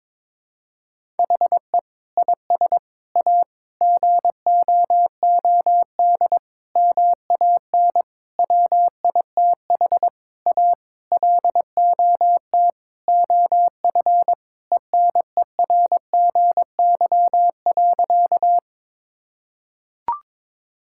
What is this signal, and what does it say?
Morse code, heard as a keyed tone, about 22 wpm, 700 Hz. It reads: HE IS A GOOD MAN WITH A LOT OF ENERGY.
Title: Sentences from Top 1000 Words - Review Part 6 - 22wpm